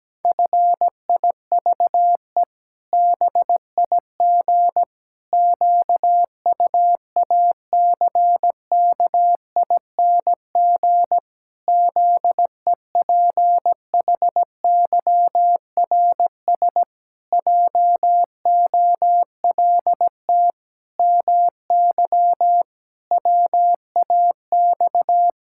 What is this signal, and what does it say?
Morse code, heard as a keyed tone, about 17 wpm, 700 Hz. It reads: FIVE BIG QUACKING ZEPHYRS JOLT MY WAX
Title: Pangram Challenge v2 - 17wpm